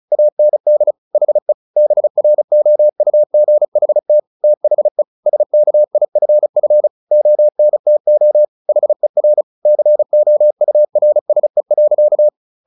Morse code, heard as a keyed tone, about 34 wpm, 600 Hz. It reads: AND HE BROUGHT THE SKIFF ONTO HER COURSE.
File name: the old man and the sea 2 35wpm